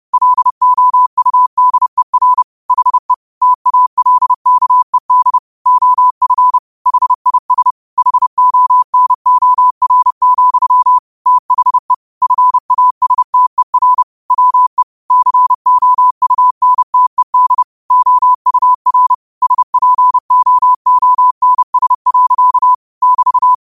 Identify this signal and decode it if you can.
Morse code, heard as a keyed tone, about 30 wpm, 1000 Hz. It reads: LOUDER HE TALKED OF HIS HONOR, THE FASTER WE COUNTED OUR SPOONS. <BT>